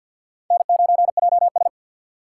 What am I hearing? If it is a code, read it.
Morse code, heard as a keyed tone, about 47 wpm, 700 Hz. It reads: N9JS
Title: USA Call Signs - 4 Characters - 3x Character Spacing - 50wpm